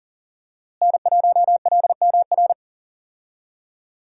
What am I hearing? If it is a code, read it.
Morse code, heard as a keyed tone, about 40 wpm, 700 Hz. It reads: N1LMR
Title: USA Call Signs - 5 Characters - Repeated 2x - 40wpm